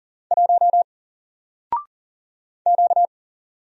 Morse code, heard as a keyed tone, about 40 wpm, 700 Hz. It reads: 1 Q
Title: Single Letter-Number Alt 40wpm